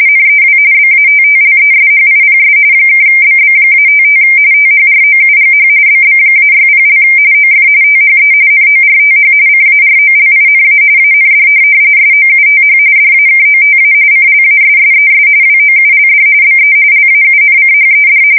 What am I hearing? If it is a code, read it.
RTTY, about 45.45 baud. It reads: NEW BATTLESHIPS (BB-55 THROUGH BB-60) OF THE NORTH CAROLINA AND SOUTH DAKOTA CLASSES WOULD BE LAUNCHED
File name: fsk170 / USS-New-Jersey